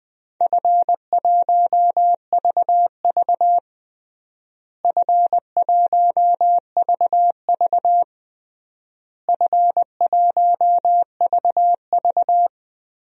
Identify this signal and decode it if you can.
Morse code, heard as a keyed tone, about 20 wpm, 690 Hz. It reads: F1VV F1VV F1VV
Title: DXCC-100 Call Signs - 4 Characters - Repeated 3x - 20wpm